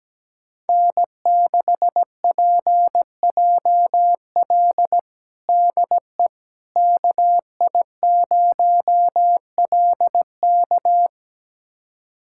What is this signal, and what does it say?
Morse code, heard as a keyed tone, about 17 wpm, 700 Hz. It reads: N6PJL DE KI0LK
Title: USA Callsign DE Callsign - 5 Characters - 17wpm